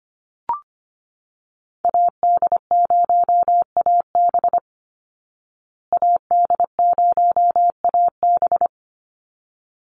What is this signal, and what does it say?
Morse code, heard as a keyed tone, about 25 wpm, 700 Hz. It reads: AD0AB AD0AB
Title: Call Signs Repeated 3x - 25wpm